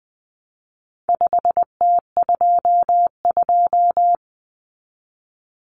Morse code, heard as a keyed tone, about 20 wpm, 700 Hz. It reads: 5T22
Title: Common Cut Numbers - 4-Digit 20wpm